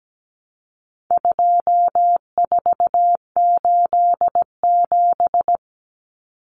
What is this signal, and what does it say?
Morse code, heard as a keyed tone, about 17 wpm, 700 Hz. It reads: 2487
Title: Numbers - 4-Digit 17wpm